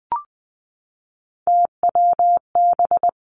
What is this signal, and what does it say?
Morse code, heard as a keyed tone, about 20 wpm, 700 Hz. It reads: TWB
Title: Three Challenging Letter-Number Combinations 20wpm